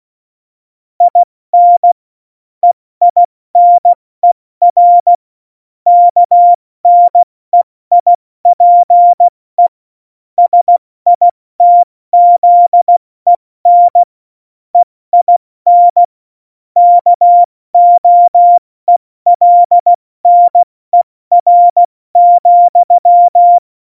Morse code, heard as a keyed tone, about 16 wpm, 700 Hz. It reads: IN EINER KNEIPE SITZEN EIN KOELNER,
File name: Morsekurs / Witze / Witz 10.Dezember 2019, 70 F80